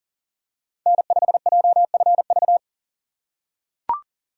Morse code, heard as a keyed tone, about 40 wpm, 700 Hz. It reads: N5JFV
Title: USA Call Signs - 4x Character Spacing - 40wpm